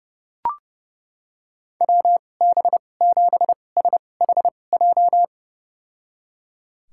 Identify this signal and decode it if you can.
Morse code, heard as a keyed tone, about 30 wpm, spaced as for 22 wpm, 700 Hz. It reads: WB7SHJ
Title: USA Call Signs - 2x Character Spacing - 30wpm